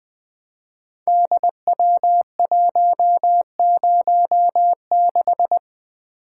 Morse code, heard as a keyed tone, about 20 wpm, 700 Hz. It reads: DW106
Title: Code Groups - 5 Letter:Numbers - 20wpm